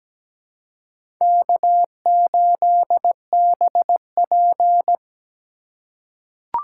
Morse code, heard as a keyed tone, about 17 wpm, 700 Hz. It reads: K8BP
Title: USA Call Signs - 2x Character Spacing - 17wpm